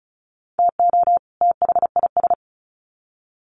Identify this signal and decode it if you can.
Morse code, heard as a keyed tone, about 34 wpm, 700 Hz. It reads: TO THIS
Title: Sets of 2 Words 35wpm